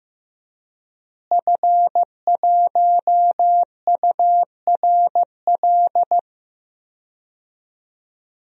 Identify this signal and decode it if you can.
Morse code, heard as a keyed tone, about 15 wpm, 700 Hz. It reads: F1URL
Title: DXCC-100 Call Signs - 5 Characters - Repeated 3x - 15wpm